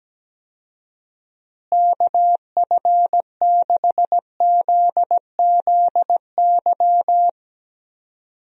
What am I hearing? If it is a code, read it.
Morse code, heard as a keyed tone, about 17 wpm, 700 Hz. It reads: KF6ZZY
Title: USA Call Signs - 6 Characters - Repeated 3x - 17wpm